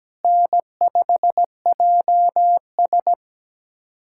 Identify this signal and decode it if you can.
Morse code, heard as a keyed tone, about 17 wpm, 700 Hz. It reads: N5JS
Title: USA Callsign DE Callsign - 4 Characters - 17wpm